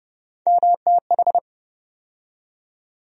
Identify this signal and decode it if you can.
Morse code, heard as a keyed tone, about 30 wpm, 700 Hz. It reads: MTH
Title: USA Call Sign Postfixes - 3 Letters 30wpm